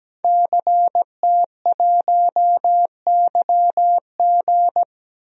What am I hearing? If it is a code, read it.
Morse code, heard as a keyed tone, about 17 wpm, 700 Hz. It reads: CT1YG
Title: DXCC-100 Call Signs - 5 Characters - 17wpm